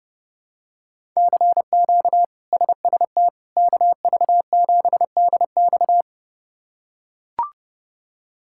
Morse code, heard as a keyed tone, about 29 wpm, 700 Hz. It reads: CQ SST KV7DX
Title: K1USN SST QSO Practice - 30wpm